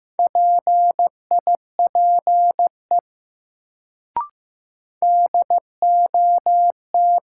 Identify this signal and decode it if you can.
Morse code, heard as a keyed tone, about 15 wpm, 700 Hz. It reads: PIPE DOT
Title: Sentences from Top 1500 Words - New Word Drill Alt - 15wpm